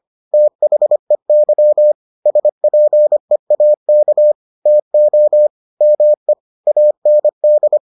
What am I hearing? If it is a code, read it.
Morse code, heard as a keyed tone, about 25 wpm, 600 Hz. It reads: THEY SPEAK TO ME AND